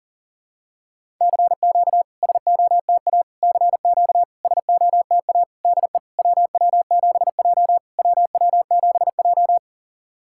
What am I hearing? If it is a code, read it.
Morse code, heard as a keyed tone, about 44 wpm, 700 Hz. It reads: CQ SOTA CQ SOTA DE WW7J WW7J
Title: SOTA QSO Practice 40wpm